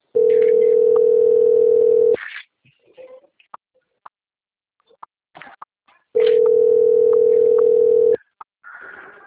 Ringback tone (North America), 440 + 480 Hz.